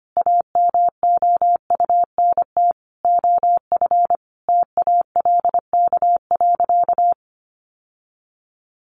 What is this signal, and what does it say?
Morse code, heard as a keyed tone, about 25 wpm, 700 Hz. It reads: AMOUNT OF TALK.
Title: Sentences from Top 1400 Words 25wpm